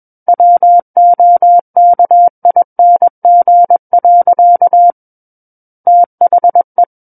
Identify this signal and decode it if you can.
Morse code, heard as a keyed tone, about 21 wpm, 700 Hz. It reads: WOKING. THE
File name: Book One / 12-Chapter 12 - 21 WPM